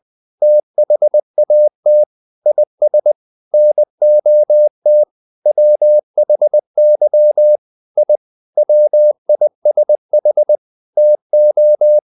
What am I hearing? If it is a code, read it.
Morse code, heard as a keyed tone, about 20 wpm, 600 Hz. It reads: THAT IS NOT WHY I WISH TO
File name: the old man and the sea part 2 20wpm